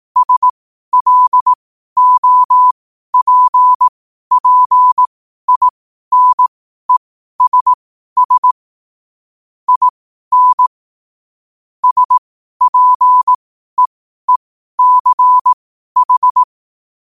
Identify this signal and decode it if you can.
Morse code, heard as a keyed tone, about 18 wpm, spaced as for 12 wpm, 1000 Hz. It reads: SLOPPINESS IN SPEECH
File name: qotdfiles / 12wpm